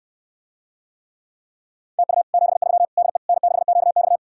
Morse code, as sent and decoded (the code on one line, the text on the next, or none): - ..- / --... ...-- / -.. / -. -.... --.- -..-
TU 73 DE N6QX